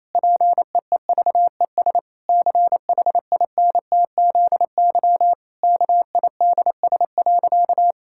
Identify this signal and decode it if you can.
Morse code, heard as a keyed tone, about 28 wpm, 700 Hz. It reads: PEEVES CHINTZY KIDS.
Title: Pangram Challenge v2 - 28wpm